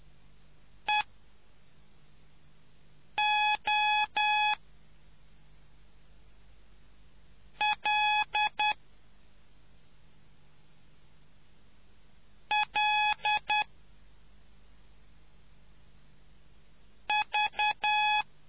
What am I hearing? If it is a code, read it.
Morse code, heard as a keyed tone, about 10 wpm, spaced as for 3 wpm, 840 Hz. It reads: EOLLV